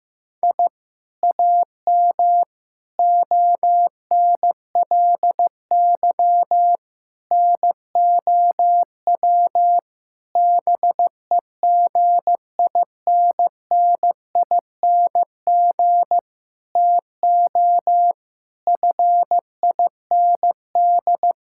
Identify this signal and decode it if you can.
Morse code, heard as a keyed tone, about 15 wpm, 700 Hz. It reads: I AM ONLY NOW BEGINNING TO FIND
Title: Sentences from Top 1000 Words - Review Part 5 - 15wpm